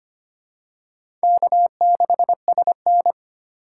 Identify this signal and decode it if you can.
Morse code, heard as a keyed tone, about 25 wpm, 700 Hz. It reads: K6SN
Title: USA Call Signs - 4 Characters - 4x Character Spacing - 25wpm